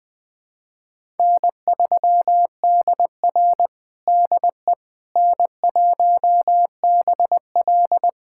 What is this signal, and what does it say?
Morse code, heard as a keyed tone, about 20 wpm, 700 Hz. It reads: N3DR DE N1BL
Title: USA Callsign DE Callsign - 4 Characters - 20wpm